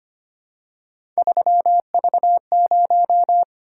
Morse code, heard as a keyed tone, about 25 wpm, 700 Hz. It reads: 3V0
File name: Three Letter-Number Combinations - 25wpm